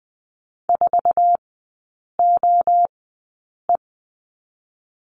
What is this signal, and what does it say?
Morse code, heard as a keyed tone, about 20 wpm, spaced as for 8 wpm, 700 Hz. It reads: 4OE
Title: 100 Lesson 17e - Letter u - Three Characters 20wpm